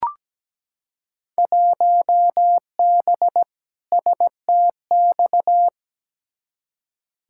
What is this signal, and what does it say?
Morse code, heard as a keyed tone, about 17 wpm, 700 Hz. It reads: 1B STX
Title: Field Day Exchanges Encore - 17wpm